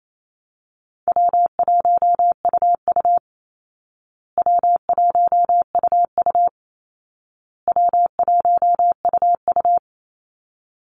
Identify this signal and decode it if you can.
Morse code, heard as a keyed tone, about 28 wpm, 700 Hz. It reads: W1UU W1UU W1UU
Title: Easy Call Signs Repeated 3x - 28wpm